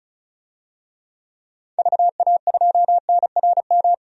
Morse code, heard as a keyed tone, about 36 wpm, 700 Hz. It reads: VA2NRM
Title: DXCC-100 Call Signs - 6 Characters - 35wpm